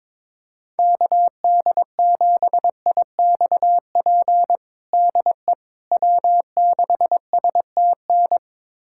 Morse code, heard as a keyed tone, about 22 wpm, 700 Hz. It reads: KD7IXP DE W6STN
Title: USA Callsign DE Callsign 22wpm